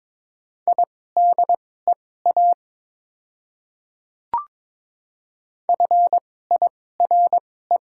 Morse code, heard as a keyed tone, about 22 wpm, spaced as for 16 wpm, 690 Hz. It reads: IDEA FIRE
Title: 2x Character Spacing - Top 500 Words - 22wpm